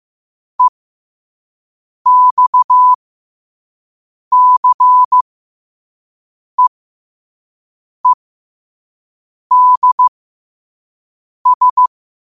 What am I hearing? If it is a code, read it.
Morse code, heard as a keyed tone, about 15 wpm, spaced as for 5 wpm, 1000 Hz. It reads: EXCEEDS
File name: qotdfiles / 5wpm